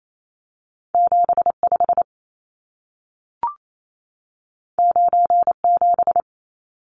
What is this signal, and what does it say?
Morse code, heard as a keyed tone, about 28 wpm, 700 Hz. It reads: 75 97
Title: Numbers - 2-Digit 28wpm